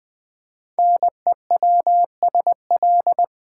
Morse code, heard as a keyed tone, about 20 wpm, 700 Hz. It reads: NEWSL
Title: Build A Word - 5 Letters At A Time - 10-Letter Words - 20wpm